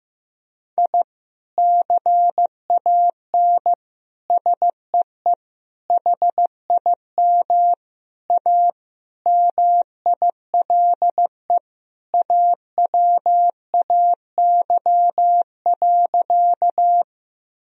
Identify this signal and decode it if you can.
Morse code, heard as a keyed tone, about 15 wpm, 700 Hz. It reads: I CAN SEE HIM A MILE AWAY.
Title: Sentences from Top 1000 Words - Review Part 1 - 15wpm